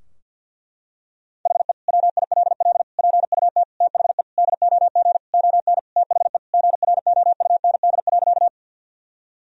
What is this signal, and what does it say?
Morse code, heard as a keyed tone, about 51 wpm, 700 Hz. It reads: HE WILL PUT THE DOG ON THE GROUND.
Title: Sentences from Top 1000 Words - Review Part 2 - 50wpm